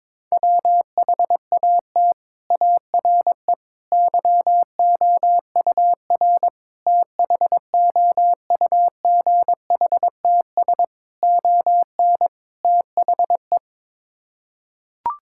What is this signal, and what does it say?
Morse code, heard as a keyed tone, about 22 wpm, 700 Hz. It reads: WHAT ARE YOUR THOUGHTS ON THE
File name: practice / 22wpm / Sets of 6 Words - 22wpm